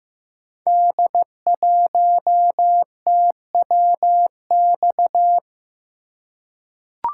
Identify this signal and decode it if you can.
Morse code, heard as a keyed tone, about 15 wpm, 700 Hz. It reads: D1TWX
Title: Code Groups - 5 Letter:Numbers - 15wpm